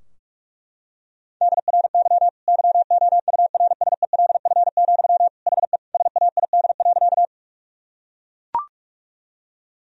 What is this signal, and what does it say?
Morse code, heard as a keyed tone, about 42 wpm, 700 Hz. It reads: DRY YOURSELF, HE SAID.
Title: Sentences from Top 1000 Words - Review Part 3 - 45wpm